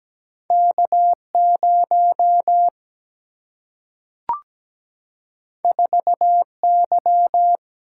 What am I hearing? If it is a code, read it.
Morse code, heard as a keyed tone, about 17 wpm, 700 Hz. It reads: K0 4Y
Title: Single-Letter Single-Number 17wpm